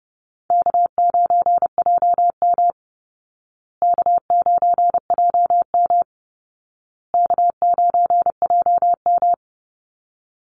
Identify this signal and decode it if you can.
Morse code, heard as a keyed tone, about 30 wpm, 700 Hz. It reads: K9JM K9JM K9JM
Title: Call Signs Repeated 2x - 30wpm